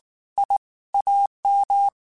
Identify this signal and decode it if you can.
Morse code, heard as a keyed tone, about 19 wpm, 800 Hz. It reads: I AM
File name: QSOs / QSOs-mp3s / WA7QOX 2@18